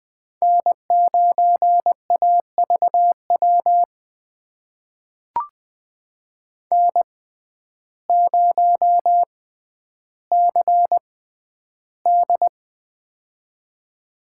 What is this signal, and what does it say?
Morse code, heard as a keyed tone, about 20 wpm, spaced as for 11 wpm, 700 Hz. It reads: N9AVW N0CD
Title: USA Call Signs - 6x Character Spacing - 20wpm